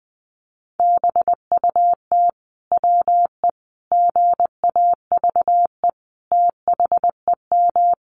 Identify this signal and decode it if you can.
Morse code, heard as a keyed tone, about 20 wpm, 700 Hz. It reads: BUT WE GAVE THEM
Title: Sentences from Top 1200 Words 20wpm